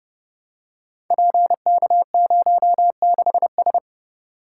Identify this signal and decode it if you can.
Morse code, heard as a keyed tone, about 28 wpm, 700 Hz. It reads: PK06S
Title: Code Groups - 5 Letter:Numbers - 30wpm